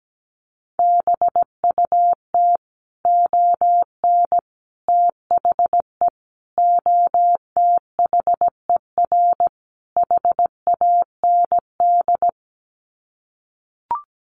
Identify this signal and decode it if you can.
Morse code, heard as a keyed tone, about 17 wpm, 700 Hz. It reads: BUT ON THE OTHER HAND
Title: Sets of 5 Words 17wpm